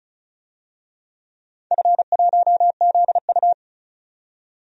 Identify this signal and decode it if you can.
Morse code, heard as a keyed tone, about 36 wpm, 700 Hz. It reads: F1ZU
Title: DXCC-100 Call Signs - 4 Characters - 35wpm